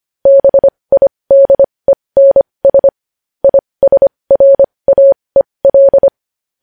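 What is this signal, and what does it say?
Morse code, heard as a keyed tone, about 25 wpm, spaced as for 20 wpm, 550 Hz. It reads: BIDENS ISRAEL
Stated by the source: Morse code news headlines sent at 20 wpm.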